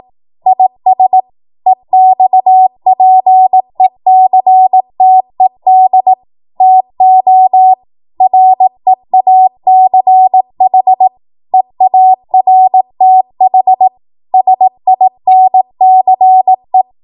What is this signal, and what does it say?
Morse code, heard as a keyed tone, about 18 wpm, 750 Hz. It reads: IS EXPECTED TO REACH EARTH SINCE
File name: Morse / 18WPM